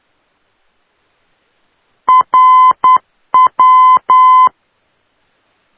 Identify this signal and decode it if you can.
Morse code, heard as a keyed tone, about 10 wpm, 1020 Hz. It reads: RW
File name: europe / d-4041-rw-392-tegel-k3-2014-05-rb